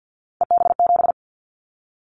Morse code, heard as a keyed tone, about 50 wpm, 700 Hz. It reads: EB7
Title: Callsign Prefixes for Top 25 DXCC - Repeated 3x - 50wpm